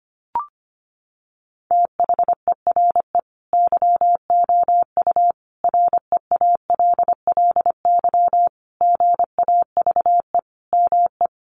Morse code, heard as a keyed tone, about 25 wpm, 700 Hz. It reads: THERE YOU REALLY GAVE ME
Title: Sentences from Top 900 Words 25wpm